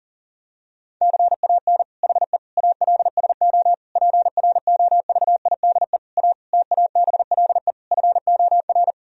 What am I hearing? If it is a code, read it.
Morse code, heard as a keyed tone, about 40 wpm, 700 Hz. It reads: CAN HE ALSO PROVIDE A TABLE FOR
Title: Sentences from Top 1000 Words - Review Part 3 - 40wpm